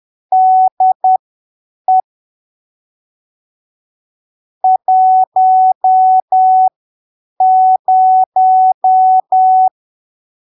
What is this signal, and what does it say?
Morse code, heard as a keyed tone, about 10 wpm, spaced as for 7 wpm, 750 Hz. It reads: DE 10
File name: arquivos / texto / texto 12 10ppm